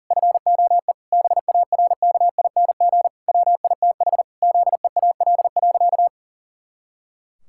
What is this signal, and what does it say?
Morse code, heard as a keyed tone, about 40 wpm, 700 Hz. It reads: FOE BARKING WITH ZEAL.
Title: Pangram Challenge v2 - 40wpm